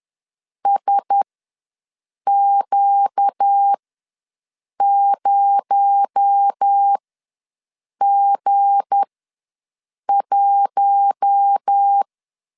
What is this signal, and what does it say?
Morse code, heard as a keyed tone, about 10 wpm, spaced as for 6 wpm, 800 Hz. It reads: SQ0G1
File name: LEZIONI / 7 GF9Z / 7 G ws 10wpm 3843